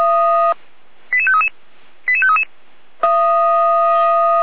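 Five-tone selective call (ZVEI), heard as quick runs of five tones, tones of about 70 ms. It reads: 88533 88533
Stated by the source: ZVEI five-tone selective call received by SDR.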